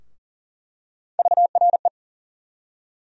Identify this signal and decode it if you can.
Morse code, heard as a keyed tone, about 40 wpm, 700 Hz. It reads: VRE
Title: USA Call Sign Postfixes - 3 Letters 40wpm